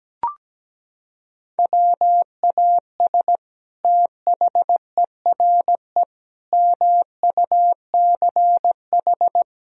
Morse code, heard as a keyed tone, about 17 wpm, 700 Hz. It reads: WAS THERE MUCH
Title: Sentences from Top 1000 Words - Review Part 4 - 17wpm